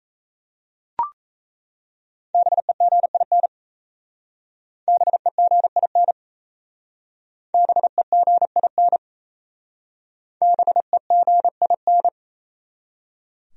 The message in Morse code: BEGIN BEGIN BEGIN BEGIN

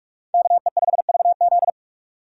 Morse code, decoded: KE5VZ

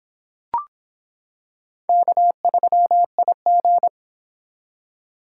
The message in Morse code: K3IG

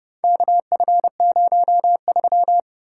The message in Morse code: KF03

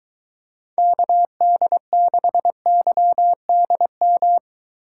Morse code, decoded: KD6YDM

